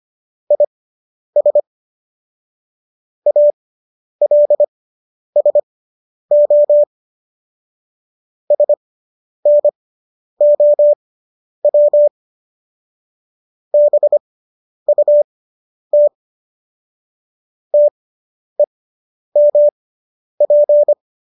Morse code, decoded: IS ALSO SNOW BUT TEMP

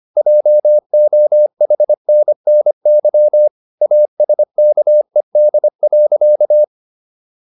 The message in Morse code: JOHNNY ASKED.

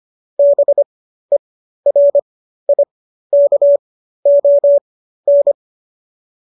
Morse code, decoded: BERIKON

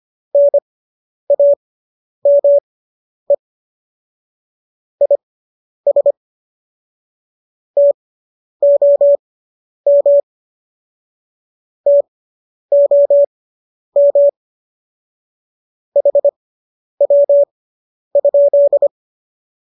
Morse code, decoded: NAME IS TOM TOM HW?

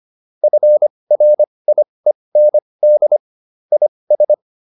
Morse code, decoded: FRIEND IS

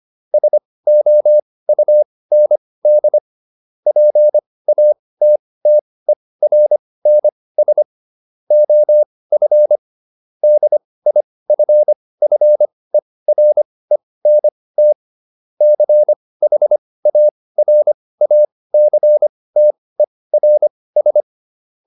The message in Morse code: SOUND PATTERNS OF DIFFERENT CHARACTERS

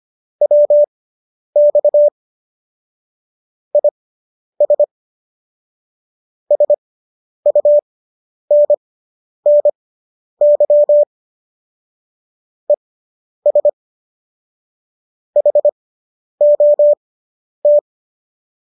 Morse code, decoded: WX IS SUNNY ES HOT

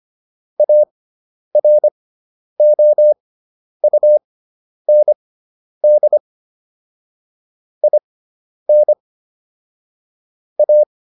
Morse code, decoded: AROUND IN A

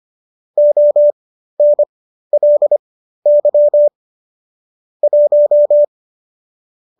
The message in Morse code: ONLY 1

